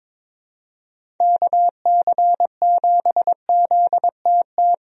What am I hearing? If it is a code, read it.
Morse code, heard as a keyed tone, about 21 wpm, 700 Hz. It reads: KC7ZTT
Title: USA Call Signs - 6 Characters - Repeated 3x - 22wpm